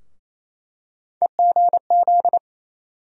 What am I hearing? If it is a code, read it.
Morse code, heard as a keyed tone, about 28 wpm, 700 Hz. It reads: EGZ
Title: USA Call Sign Postfixes - 3 Letters 28wpm